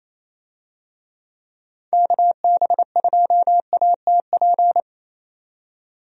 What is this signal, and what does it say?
Morse code, heard as a keyed tone, about 28 wpm, 700 Hz. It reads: KB2ATP
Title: USA Call Signs - 6 Characters - 28wpm